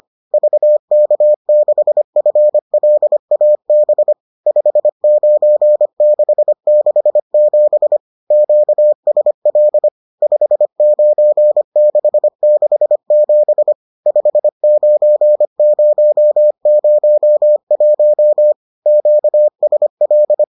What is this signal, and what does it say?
Morse code, heard as a keyed tone, about 25 wpm, 600 Hz. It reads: VK6FLAB 59667 QSL 59667 59001 QSL